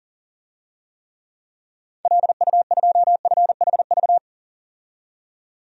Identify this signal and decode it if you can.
Morse code, heard as a keyed tone, about 40 wpm, 700 Hz. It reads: LU2FHV
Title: DXCC-100 Call Signs - 6 Characters - 40wpm